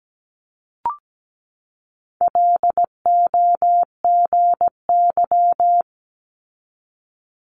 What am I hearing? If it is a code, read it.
Morse code, heard as a keyed tone, about 17 wpm, 700 Hz. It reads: LOGY
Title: Multi-letter Phonemes 17wpm